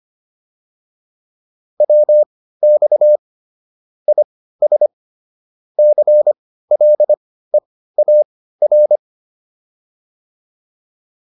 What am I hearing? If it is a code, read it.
Morse code, heard as a keyed tone, about 25 wpm, spaced as for 15 wpm, 600 Hz. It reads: WX IS CLEAR